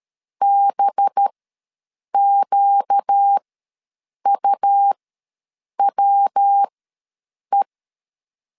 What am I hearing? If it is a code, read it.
Morse code, heard as a keyed tone, about 13 wpm, spaced as for 7 wpm, 800 Hz. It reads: BQUWE
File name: LEZIONI / 5 LBVC / 5 LB ws 12wpm 3034